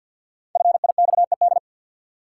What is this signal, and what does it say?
Morse code, heard as a keyed tone, about 52 wpm, 700 Hz. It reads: FIXED
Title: Sentences from Top 1100 Words - New Word Drill Alt - 50wpm